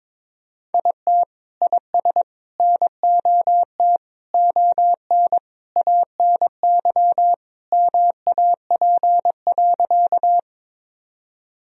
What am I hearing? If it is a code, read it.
Morse code, heard as a keyed tone, about 22 wpm, 700 Hz. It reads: IT IS NOT ON ANY MAP.
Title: Sentences from Top 1000 Words - Review Part 2 - 22wpm